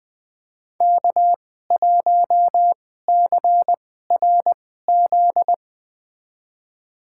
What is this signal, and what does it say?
Morse code, heard as a keyed tone, about 20 wpm, spaced as for 15 wpm, 700 Hz. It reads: K1CRZ
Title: USA Call Signs - 5 Characters - 2x Character Spacing - 20wpm